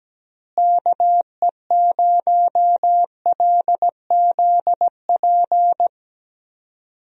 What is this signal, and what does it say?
Morse code, heard as a keyed tone, about 17 wpm, 700 Hz. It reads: KE0LZP
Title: USA Call Signs - 6 Characters - Repeated 3x - 17wpm